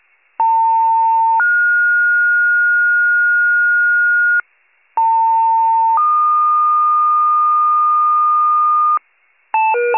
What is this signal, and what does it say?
Two-tone page, heard as a one-second tone, then a three-second tone, about 900 then 1430 Hz, about 900 then 1220 Hz, with noise. It is heard from 0.4 to 9.0 s.